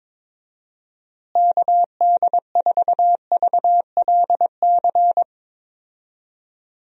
Morse code, heard as a keyed tone, about 21 wpm, 700 Hz. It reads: KD4VLC
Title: USA Call Signs - 6 Characters - Repeated 2x - 22wpm